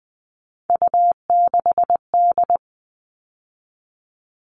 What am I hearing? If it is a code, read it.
Morse code, heard as a keyed tone, about 20 wpm, 700 Hz. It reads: U6D
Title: All Cut Numbers - 3-Digit 20wpm